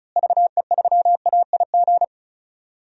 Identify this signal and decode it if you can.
Morse code, heard as a keyed tone, about 34 wpm, 700 Hz. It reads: VE3AIG
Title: DXCC-100 Call Signs - 6 Characters - Repeated 3x - 35wpm